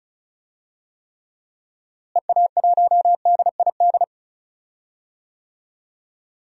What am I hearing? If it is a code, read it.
Morse code, heard as a keyed tone, about 36 wpm, 700 Hz. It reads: EA1DID
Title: DXCC-100 Call Signs - 6 Characters - Repeated 2x - 35wpm